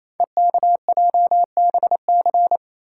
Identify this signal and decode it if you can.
Morse code, heard as a keyed tone, about 29 wpm, 700 Hz. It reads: EKJBC
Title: Code Groups - 5 Letters - 28wpm